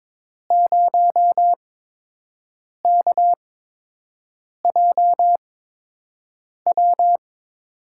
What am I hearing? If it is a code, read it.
Morse code, heard as a keyed tone, about 22 wpm, spaced as for 6 wpm, 700 Hz. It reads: 0KJW
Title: USA Call Signs - 8x Character Spacing - 22wpm